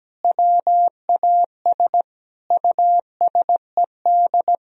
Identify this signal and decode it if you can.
Morse code, heard as a keyed tone, about 17 wpm, 700 Hz. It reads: WAS USED
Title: Sentences from Top 1000 Words - Review Part 4 - 17wpm